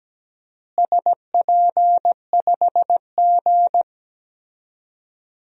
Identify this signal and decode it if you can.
Morse code, heard as a keyed tone, about 17 wpm, 700 Hz. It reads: SP5G
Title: DXCC-100 Call Signs - 4 Characters - Repeated 3x - 17wpm